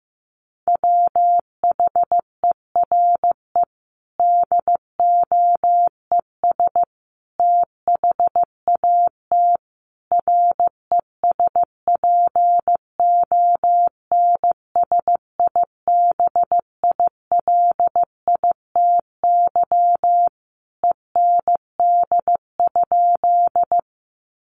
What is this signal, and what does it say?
Morse code, heard as a keyed tone, about 15 wpm, 700 Hz. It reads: WHERE DOES THAT RESPONSIBILITY END?
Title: Sentences from Top 1200 Words 15wpm